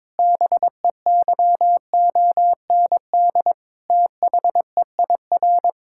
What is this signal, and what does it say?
Morse code, heard as a keyed tone, about 22 wpm, 700 Hz. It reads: BEYOND THEIR
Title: Sentences from Top 1000 Words - Review Part 6 - 22wpm